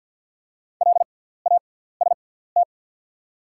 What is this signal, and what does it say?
Morse code, heard as a keyed tone, about 48 wpm, spaced as for 17 wpm, 700 Hz. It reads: LAST
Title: 6x Character Spacing - Top 300 Words - 50wpm